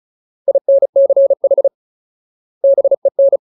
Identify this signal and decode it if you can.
Morse code, heard as a keyed tone, about 34 wpm, 550 Hz. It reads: INCH BEN
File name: Ultimate Word List / 35WPM Ultimate